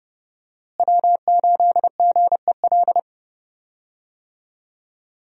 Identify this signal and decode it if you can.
Morse code, heard as a keyed tone, about 30 wpm, 700 Hz. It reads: W8GEL
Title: USA Call Signs - 5 Characters - Repeated 2x - 30wpm